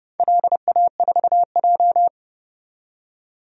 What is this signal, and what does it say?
Morse code, heard as a keyed tone, about 30 wpm, 700 Hz. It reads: LA4J
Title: DXCC-100 Call Signs - 4 Characters - 30wpm